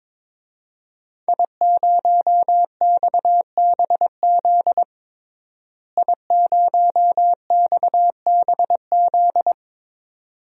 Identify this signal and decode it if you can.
Morse code, heard as a keyed tone, about 22 wpm, 700 Hz. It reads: I0XBZ I0XBZ
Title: DXCC-100 Call Signs - 5 Characters - Repeated 2x - 22wpm